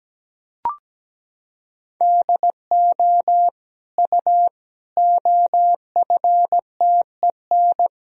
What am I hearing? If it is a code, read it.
Morse code, heard as a keyed tone, about 17 wpm, 700 Hz. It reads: DO U OFTEN
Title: Sentences from Top 300 Words and Abbreviations 17wpm